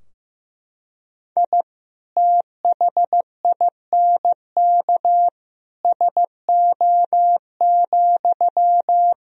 Morse code, heard as a keyed tone, about 15 wpm, 700 Hz. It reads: I THINK SO,
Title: Sentences from Top 1000 Words - Review Part 3 - 15wpm